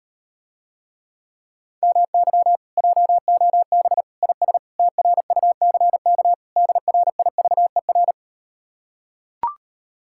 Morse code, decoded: MY JOB IS TRUCK DRIVER

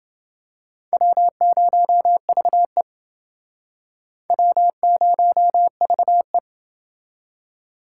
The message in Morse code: W0VE W0VE

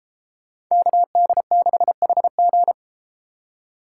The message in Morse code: KD6HG

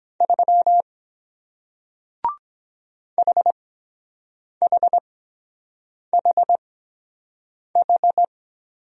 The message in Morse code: 3 HHHH

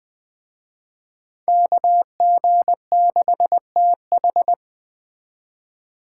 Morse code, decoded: KG6TH

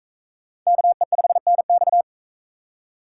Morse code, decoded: KE5NX

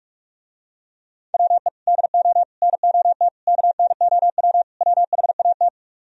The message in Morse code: WE DO NOT KNOW WHAT

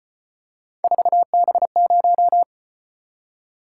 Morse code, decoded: 4B0

